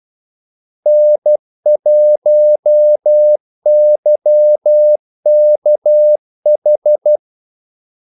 Morse code, heard as a keyed tone, about 12 wpm, 600 Hz. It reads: N1YKH
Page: 2 ekstra ordmellemrum mellem kaldesignalerne